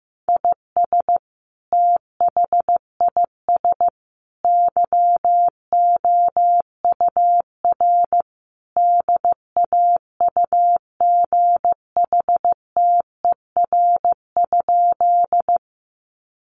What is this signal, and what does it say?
Morse code, heard as a keyed tone, about 15 wpm, 700 Hz. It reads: IS THIS YOUR DAUGHTER?
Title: Sentences from Top 1500 Words 15wpm